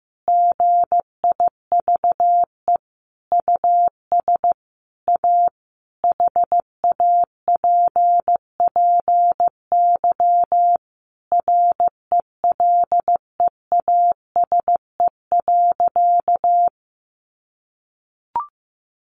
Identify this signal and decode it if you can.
Morse code, heard as a keyed tone, about 15 wpm, 700 Hz. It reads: GIVE US A HAPPY RELEASE.
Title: Sentences from Top 1100 Words 15wpm